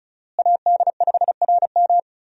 Morse code, heard as a keyed tone, about 34 wpm, 700 Hz. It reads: AD5RM
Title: USA Call Signs - 5 Characters - Repeated 2x - 35wpm